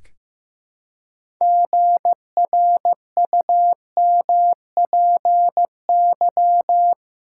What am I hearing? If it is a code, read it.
Morse code, heard as a keyed tone, about 16 wpm, 700 Hz. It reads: GRUMPY
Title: Pangram Challenge v2 - 15wpm